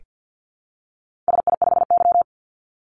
Morse code, heard as a keyed tone, about 48 wpm, 700 Hz. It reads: SI5Y
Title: Call Signs - 50wpm